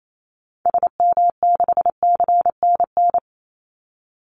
Morse code, heard as a keyed tone, about 28 wpm, 700 Hz. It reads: SM6CNN
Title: Call Signs Repeated 3x - 28wpm